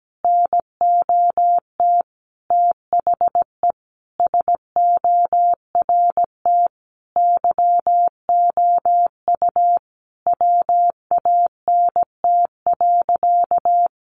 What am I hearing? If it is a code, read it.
Morse code, heard as a keyed tone, about 17 wpm, 700 Hz. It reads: NOT THE SORT YOU WANT.
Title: Sentences from Top 1300 Words 17wpm